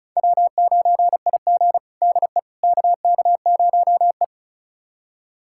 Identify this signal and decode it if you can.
Morse code, heard as a keyed tone, about 34 wpm, 700 Hz. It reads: W9IG DE KK0E
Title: USA Callsign DE Callsign - 4 Characters - 35wpm